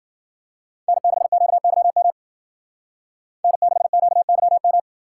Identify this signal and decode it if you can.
Morse code, heard as a keyed tone, about 48 wpm, 700 Hz. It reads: N6QYM N6QYM
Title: Speed Racing - USA Call Signs 40wpm